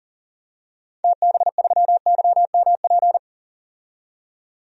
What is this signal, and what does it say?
Morse code, heard as a keyed tone, about 40 wpm, 700 Hz. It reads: TB3YMP
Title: DXCC-100 Call Signs - 6 Characters - 40wpm